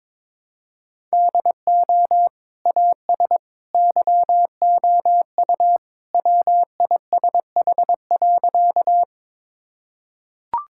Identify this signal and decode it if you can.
Morse code, heard as a keyed tone, about 22 wpm, 700 Hz. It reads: DO AS YOU WISH.
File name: practice / 22wpm / Sentences with 2-Word Phrases - 22wpm